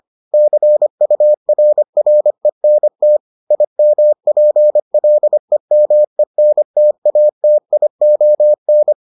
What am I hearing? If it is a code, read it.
Morse code, heard as a keyed tone, about 25 wpm, 600 Hz. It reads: CURRENT IMPLEMENTATION